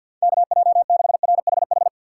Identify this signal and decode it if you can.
Morse code, heard as a keyed tone, about 46 wpm, 700 Hz. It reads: KJ6RHH
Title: USA Call Signs - 6 Characters - 4x Character Spacing - 50wpm